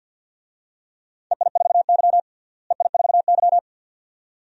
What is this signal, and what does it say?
Morse code, heard as a keyed tone, about 46 wpm, 700 Hz. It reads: EI4Y EI4Y
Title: DXCC-100 Call Signs - 4 Characters - Repeated 2x - 50wpm